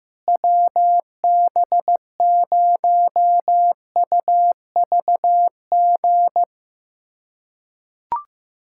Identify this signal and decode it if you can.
Morse code, heard as a keyed tone, about 16 wpm, 700 Hz. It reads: WB0UVG
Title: USA Callsign DE Callsign - 6 Characters - 15wpm